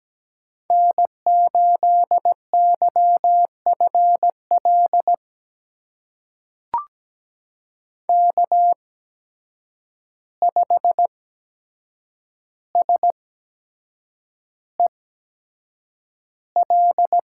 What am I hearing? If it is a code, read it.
Morse code, heard as a keyed tone, about 17 wpm, spaced as for 7 wpm, 700 Hz. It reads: N8YFL K5SEL
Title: USA Call Signs - 5 Characters - 8x Character Spacing - 17wpm